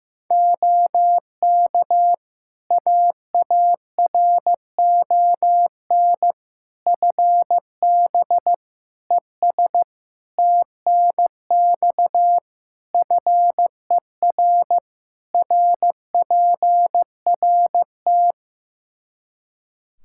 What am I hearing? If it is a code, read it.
Morse code, heard as a keyed tone, about 15 wpm, 700 Hz. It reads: OK AARON FB ES TNX FER RPRT
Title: K1USN SST QSO Practice 15wpm